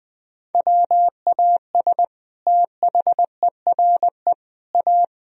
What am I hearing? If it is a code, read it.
Morse code, heard as a keyed tone, about 20 wpm, 700 Hz. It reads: WAS THERE A
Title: Sentences from Top 1000 Words - Review Part 4 - 20wpm